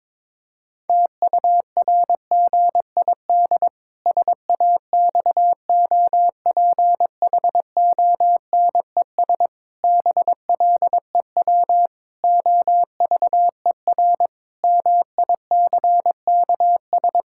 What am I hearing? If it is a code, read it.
Morse code, heard as a keyed tone, about 22 wpm, 700 Hz. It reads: TURGID SAXOPHONES BLEW OVER MICKS
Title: Pangram Challenge v2 - 22wpm